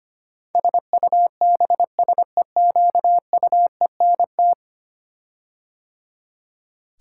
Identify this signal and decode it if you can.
Morse code, heard as a keyed tone, about 25 wpm, 700 Hz. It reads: SUBSEQUENT
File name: 10 Letter Words - First 3 Letters Given - 25wpm